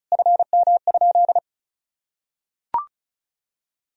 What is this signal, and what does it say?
Morse code, heard as a keyed tone, about 34 wpm, 700 Hz. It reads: FM?
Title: Sentences from Top 400 Words and Abbreviations 35wpm